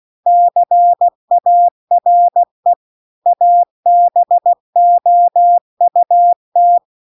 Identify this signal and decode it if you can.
Morse code, heard as a keyed tone, about 16 wpm, 700 Hz. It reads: CARE ABOUT